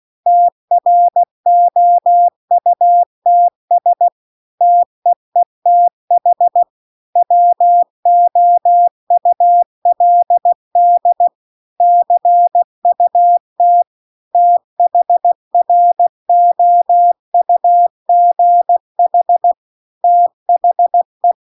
Morse code, heard as a keyed tone, about 16 wpm, 700 Hz. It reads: TROUTS TEETH WOULD CUT THROUGH THE